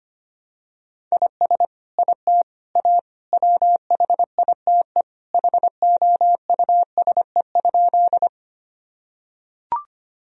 Morse code, heard as a keyed tone, about 25 wpm, 700 Hz. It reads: IS IT A WHITE HOUSE?
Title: Sentences from Top 1000 Words - Review Part 1 - 25wpm